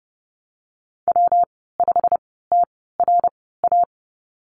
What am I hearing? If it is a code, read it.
Morse code, heard as a keyed tone, about 30 wpm, spaced as for 17 wpm, 700 Hz. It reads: W5TRA
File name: Lesson 19g - Letter w - Call Signs 30wpm